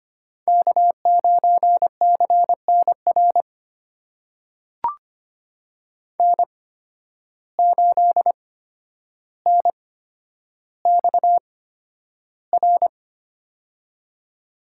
Morse code, heard as a keyed tone, about 25 wpm, spaced as for 11 wpm, 700 Hz. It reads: K9CNR N8NXR
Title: USA Call Signs - 5 Characters - 8x Character Spacing - 25wpm